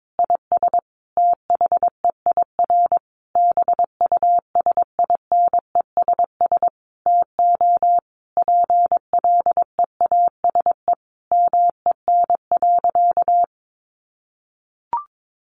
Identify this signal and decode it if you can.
Morse code, heard as a keyed tone, about 22 wpm, 700 Hz. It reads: IS THEIR BUSINESS TO PLEASE MEN.
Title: Sentences from Top 1500 Words 22wpm